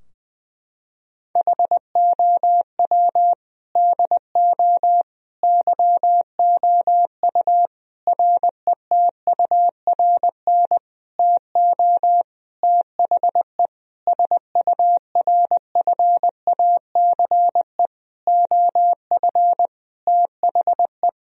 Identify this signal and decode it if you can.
Morse code, heard as a keyed tone, about 19 wpm, 700 Hz. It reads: HOW DO YOU RETURN TO THE SURFACE OF THE
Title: Sentences from Top 1000 Words - Review Part 3 - 20wpm